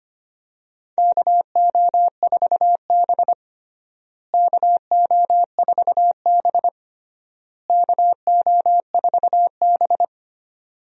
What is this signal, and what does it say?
Morse code, heard as a keyed tone, about 25 wpm, 700 Hz. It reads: KO4B KO4B KO4B
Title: USA Call Signs - 4 Characters - Repeated 3x - 25wpm